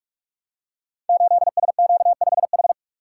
Morse code, heard as a keyed tone, about 44 wpm, 700 Hz. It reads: 8SQ5H
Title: Code Groups - 5 Letter:Numbers - 45wpm